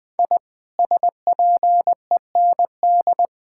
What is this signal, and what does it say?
Morse code, heard as a keyed tone, about 19 wpm, 700 Hz. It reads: I SPEND
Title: Sentences with 5-Word Phrases - Spoken Beginning - 20wpm